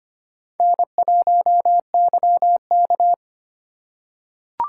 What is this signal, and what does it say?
Morse code, heard as a keyed tone, about 25 wpm, 700 Hz. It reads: N1YK
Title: USA Call Signs - 4 Characters - 2x Character Spacing - 25wpm